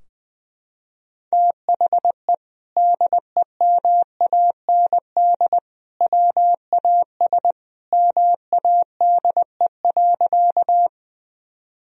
Morse code, heard as a keyed tone, about 20 wpm, 700 Hz. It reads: THE DEMAND WAS MADE.
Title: Sentences from Top 1000 Words - Review Part 4 - 20wpm